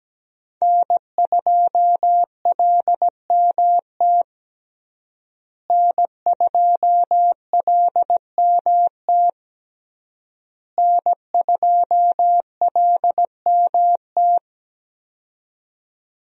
Morse code, heard as a keyed tone, about 17 wpm, 700 Hz. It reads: N2LMT N2LMT N2LMT
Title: USA Call Signs - 5 Characters - Repeated 3x - 17wpm